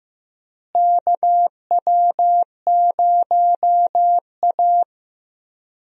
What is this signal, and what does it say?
Morse code, heard as a keyed tone, about 15 wpm, 700 Hz. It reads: KW0A
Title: USA Call Signs - 4 Characters - Repeated 3x - 15wpm